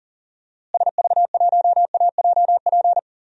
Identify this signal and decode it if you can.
Morse code, heard as a keyed tone, about 40 wpm, 700 Hz. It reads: SV1AJP
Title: DXCC-100 Call Signs - 6 Characters - Repeated 3x - 40wpm